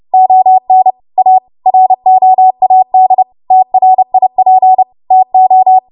Morse code, heard as a keyed tone, about 30 wpm, 750 Hz. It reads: ON A ROAD TRIP TO
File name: Morse / 30WPM